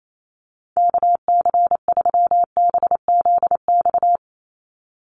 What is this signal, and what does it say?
Morse code, heard as a keyed tone, about 28 wpm, 700 Hz. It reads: KC3BZX
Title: USA Call Signs 28wpm